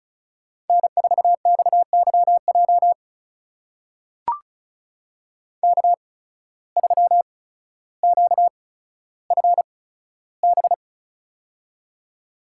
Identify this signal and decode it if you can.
Morse code, heard as a keyed tone, about 36 wpm, spaced as for 15 wpm, 690 Hz. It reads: N4XYJ K3QFB
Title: USA Call Signs - 5 Characters - 8x Character Spacing - 35wpm